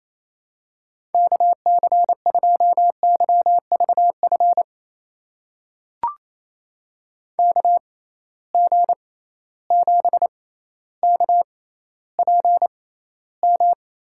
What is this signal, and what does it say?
Morse code, heard as a keyed tone, about 28 wpm, spaced as for 14 wpm, 700 Hz. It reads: KC2YVF KG7KPM
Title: USA Call Signs - 6 Characters - 6x Character Spacing - 28wpm